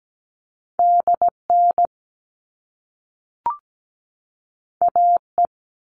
Morse code, heard as a keyed tone, about 17 wpm, 700 Hz. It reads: DN AE
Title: All Cut Numbers - 2-Digit 17wpm